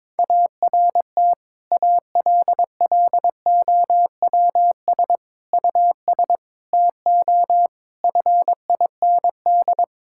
Morse code, heard as a keyed tone, about 22 wpm, 700 Hz. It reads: ART ALLOWS US TO FIND
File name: Difficult Sentences with Hints - 22wpm